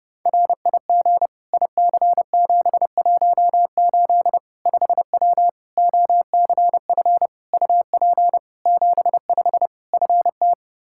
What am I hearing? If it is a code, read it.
Morse code, heard as a keyed tone, about 30 wpm, 700 Hz. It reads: RIG IC718 5W OCF UP 75 FT
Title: K1USN SST QSO Practice 30wpm